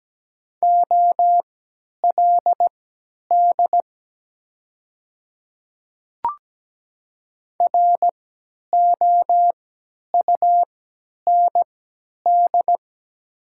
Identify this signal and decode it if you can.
Morse code, heard as a keyed tone, about 17 wpm, spaced as for 10 wpm, 700 Hz. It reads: OLD ROUND
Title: 3x Character Spacing - Top 200 Words - 17wpm